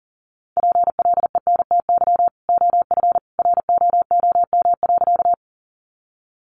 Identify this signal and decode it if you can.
Morse code, heard as a keyed tone, about 40 wpm, 700 Hz. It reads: PLENTY OF ROOM.
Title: Sentences from Top 1300 Words 40wpm